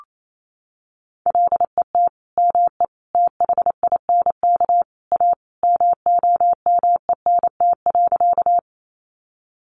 Morse code, heard as a keyed tone, about 28 wpm, 700 Hz. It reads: LET ME THINK A MOMENT.
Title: Sentences from Top 1000 Words 28wpm